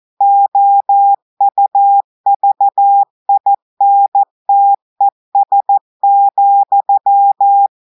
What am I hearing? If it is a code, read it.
Morse code, heard as a keyed tone, about 14 wpm, 800 Hz. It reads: OUVINTES,
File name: audios / portugais / portugais texte en clair 2 vitesse 14 mots minutes